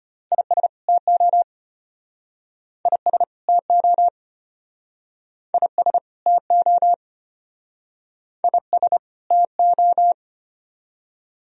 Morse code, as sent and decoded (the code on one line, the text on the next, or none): .. ... / - --- / .. ... / - --- / .. ... / - --- / .. ... / - ---
IS TO IS TO IS TO IS TO